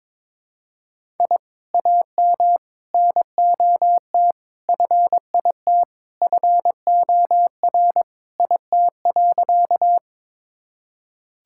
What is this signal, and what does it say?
Morse code, heard as a keyed tone, about 22 wpm, 700 Hz. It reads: I AM NOT FIT FOR IT.
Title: Sentences from Top 1000 Words - Review Part 5 - 22wpm